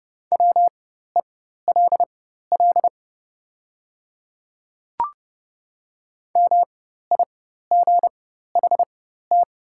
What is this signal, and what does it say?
Morse code, heard as a keyed tone, about 31 wpm, spaced as for 14 wpm, 700 Hz. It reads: WELL MIGHT
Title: 4x Character Spacing - Top 300 Words - 30wpm